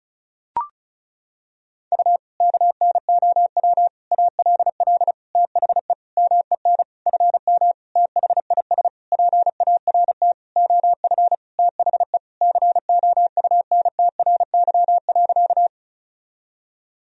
Morse code, heard as a keyed tone, about 34 wpm, 700 Hz. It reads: U KNOW ALL THE MEN FM THIS PART OF THE COUNTRY.
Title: Sentences from Top 300 Words and Abbreviations 35wpm